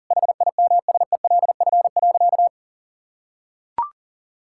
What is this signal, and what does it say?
Morse code, heard as a keyed tone, about 40 wpm, 700 Hz. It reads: HIMSELF.
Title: Sentences from Top 1600 Words 40wpm